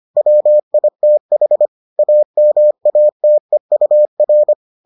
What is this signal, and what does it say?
Morse code, heard as a keyed tone, about 25 wpm, 600 Hz. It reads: WITH AMATEUR